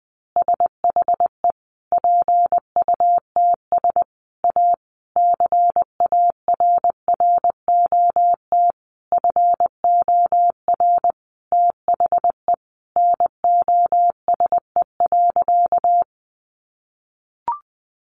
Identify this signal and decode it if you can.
Morse code, heard as a keyed tone, about 20 wpm, 700 Hz. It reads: SHE PUTS A CARROT FOR THE NOSE.
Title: Easy Phrases 20wpm